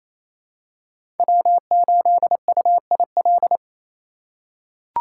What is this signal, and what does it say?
Morse code, heard as a keyed tone, about 27 wpm, 700 Hz. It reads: W8UIL
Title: K1USN SST QSO Practice - 28wpm